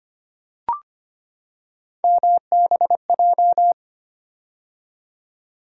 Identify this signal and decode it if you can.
Morse code, heard as a keyed tone, about 25 wpm, 700 Hz. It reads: MBJ
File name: Three Letter-Number Combinations - 25wpm